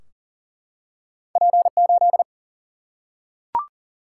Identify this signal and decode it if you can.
Morse code, heard as a keyed tone, about 41 wpm, 700 Hz. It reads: P8
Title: Single-Letter Single-Number 40wpm